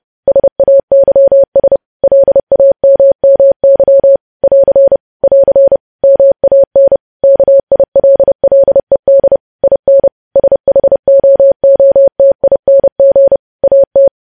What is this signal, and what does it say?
Morse code, heard as a keyed tone, about 30 wpm, 550 Hz. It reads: SAYS LAMMY <AR> <AR> MAN KILLED IN SHOOTING AT
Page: news headlines in morse code at 30 wpm